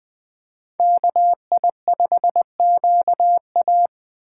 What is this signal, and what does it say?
Morse code, heard as a keyed tone, about 24 wpm, 700 Hz. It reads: KI5QA
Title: Speed Racing - USA Call Signs 20wpm